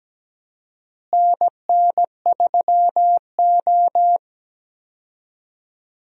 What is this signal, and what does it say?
Morse code, heard as a keyed tone, about 17 wpm, 700 Hz. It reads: NN3O
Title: USA Call Signs - 4 Characters - Repeated 3x - 17wpm